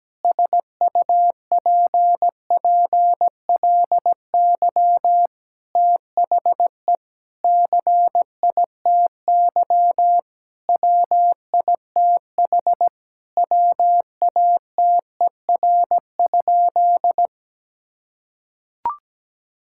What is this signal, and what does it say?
Morse code, heard as a keyed tone, about 17 wpm, 700 Hz. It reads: SUPPLY THE CITY WITH WATER?
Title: Sentences from Top 1000 Words - Review Part 5 - 17wpm